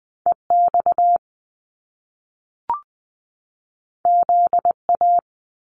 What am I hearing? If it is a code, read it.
Morse code, heard as a keyed tone, about 20 wpm, 700 Hz. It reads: EX ZA
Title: Top Two Letter Combinations in Words 20wpm